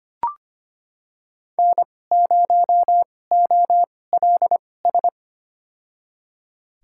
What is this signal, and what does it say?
Morse code, heard as a keyed tone, about 25 wpm, spaced as for 18 wpm, 700 Hz. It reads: N0OLS
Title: USA Call Signs - 5 Characters - 2x Character Spacing - 25wpm